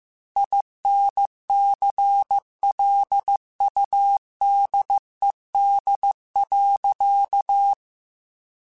Morse code, heard as a keyed tone, about 15 wpm, 800 Hz. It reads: INCLUDED.
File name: Chuckles / Chuckles-mp3s / Guaranteed @13